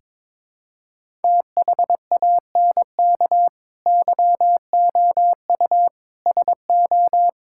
Morse code, heard as a keyed tone, about 22 wpm, 700 Hz. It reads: THANK YOU SO
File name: practice / 22wpm / Sentences with 2-Word Phrases - 22wpm